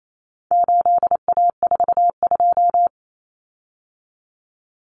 Morse code, heard as a keyed tone, about 28 wpm, 700 Hz. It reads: 8A42
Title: Common Cut Numbers - 4-Digit 28wpm